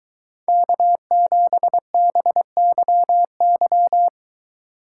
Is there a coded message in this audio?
Yes: Morse code from 0.5 to 4.1 s, heard as a keyed tone, about 23 wpm, 700 Hz. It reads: K7BYY